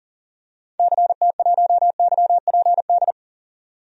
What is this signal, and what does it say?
Morse code, heard as a keyed tone, about 39 wpm, 700 Hz. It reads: CT1YPD
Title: DXCC-100 Call Signs - 6 Characters - 40wpm